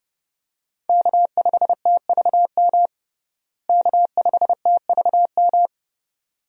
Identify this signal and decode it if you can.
Morse code, heard as a keyed tone, about 30 wpm, 700 Hz. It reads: K5TVM K5TVM
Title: USA Call Signs - 5 Characters - Repeated 2x - 30wpm